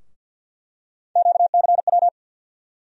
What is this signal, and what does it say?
Morse code, heard as a keyed tone, about 51 wpm, 700 Hz. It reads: QCW
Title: USA Call Sign Postfixes - 3 Letters 50wpm